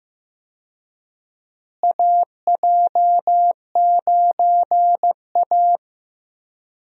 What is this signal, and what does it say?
Morse code, heard as a keyed tone, about 15 wpm, 700 Hz. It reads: AJ9A
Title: USA Call Signs - 4 Characters - Repeated 3x - 15wpm